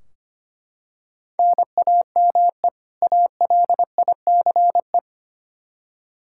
Morse code, heard as a keyed tone, about 25 wpm, 700 Hz. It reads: NAME ALICE
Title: Practice QSO - Exchange 1 - Beginning - 25wpm